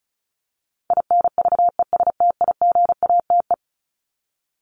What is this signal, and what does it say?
Morse code, heard as a keyed tone, about 34 wpm, 700 Hz. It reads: INVESTIGATE
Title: Top 1751-2000 Words 35wpm